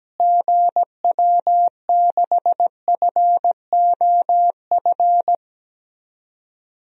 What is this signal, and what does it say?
Morse code, heard as a keyed tone, about 17 wpm, 700 Hz. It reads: GW6FOF
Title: DXCC-100 Call Signs - 6 Characters - Repeated 2x - 17wpm